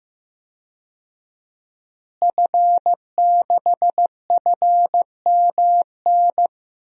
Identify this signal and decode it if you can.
Morse code, heard as a keyed tone, about 15 wpm, 700 Hz. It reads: F6FMN
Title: DXCC-100 Call Signs - 5 Characters - Repeated 3x - 15wpm